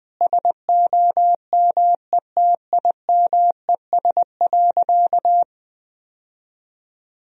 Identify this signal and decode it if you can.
Morse code, heard as a keyed tone, about 20 wpm, 700 Hz. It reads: SOMETIMES.
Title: Sentences from Top 1000 Words - Review Part 3 - 20wpm